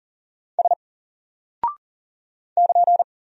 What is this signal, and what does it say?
Morse code, heard as a keyed tone, about 40 wpm, 700 Hz. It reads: S <KN>
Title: Single Character (Letter:Number:Punctuation:Prosign) Alt 40wpm